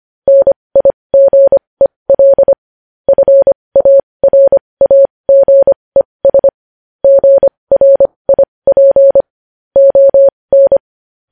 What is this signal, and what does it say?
Morse code, heard as a keyed tone, about 25 wpm, spaced as for 20 wpm, 550 Hz. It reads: NIGEL FARAGES GRIP ON
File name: podcast / News headlines-1x-25x20